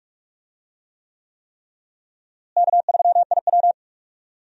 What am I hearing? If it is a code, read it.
Morse code, heard as a keyed tone, about 46 wpm, 700 Hz. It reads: K3IW
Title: USA Call Signs - 4 Characters - Repeated 3x - 45wpm